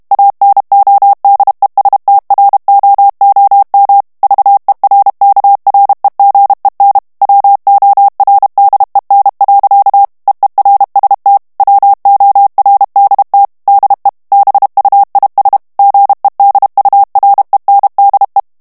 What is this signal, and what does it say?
Morse code, heard as a keyed tone, about 32 wpm, 800 Hz. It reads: ANODESTROOM VERKREGEN WORDEN. EERST WORDT DE BUIS GEDURENDE
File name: HOLLANDAIS texte clair 2 vitesse 30 mots minute